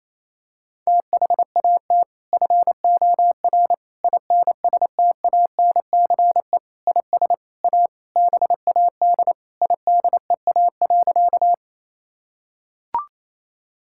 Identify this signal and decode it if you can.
Morse code, heard as a keyed tone, about 27 wpm, 700 Hz. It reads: THAT FOR INSTANCE IS A BAD IDEA.
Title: Sentences from Top 1000 Words - Review Part 5 - 28wpm